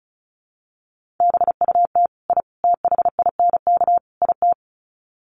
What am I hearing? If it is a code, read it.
Morse code, heard as a keyed tone, about 34 wpm, 700 Hz. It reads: BUT I THINK IT
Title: Sets of 4 Words 35wpm